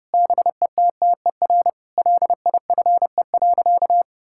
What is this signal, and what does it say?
Morse code, heard as a keyed tone, about 30 wpm, 700 Hz. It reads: BETTER LIFE.
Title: Sentences from Top 1000 Words - Review Part 5 - 30wpm